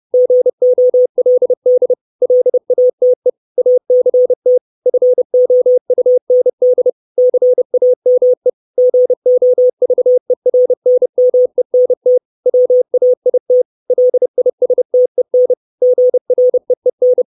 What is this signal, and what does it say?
Morse code, heard as a keyed tone, about 30 wpm, 500 Hz. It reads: GOLD LATE ACT FOUND CAME GOVERNMENT WAIT LISTEN GREEN
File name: 632 Words 30WPM